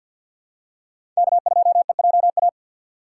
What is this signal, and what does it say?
Morse code, heard as a keyed tone, about 52 wpm, 700 Hz. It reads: K2EJA
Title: USA Call Signs - 5 Characters - 4x Character Spacing - 50wpm